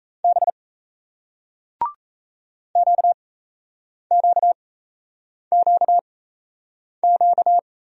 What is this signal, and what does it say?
Morse code, heard as a keyed tone, about 40 wpm, spaced as for 8 wpm, 700 Hz. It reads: B QQQQ